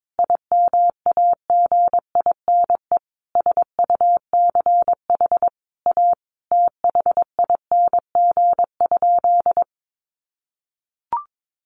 Morse code, heard as a keyed tone, about 22 wpm, 700 Hz. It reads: IMAGINE SUCH A THING?
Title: Sentences from Top 1400 Words 22wpm